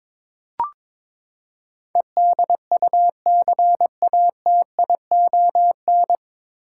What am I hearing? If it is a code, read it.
Morse code, heard as a keyed tone, about 21 wpm, 700 Hz. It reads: EDUCATION
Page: Morse Code Ninja practice 22wpm